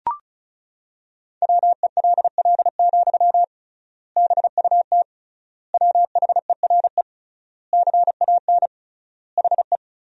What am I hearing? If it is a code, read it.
Morse code, heard as a keyed tone, about 36 wpm, 700 Hz. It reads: WELL, BUT WHERE CAN HE
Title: 3x Word-Spacing - Sentences from Top 500 Words - 35wpm